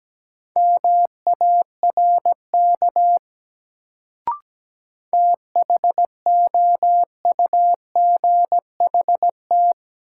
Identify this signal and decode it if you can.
Morse code, heard as a keyed tone, about 17 wpm, 700 Hz. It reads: MARK THOUGHT
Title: Sentences from Top 300 Words - New Word Drill - 17wpm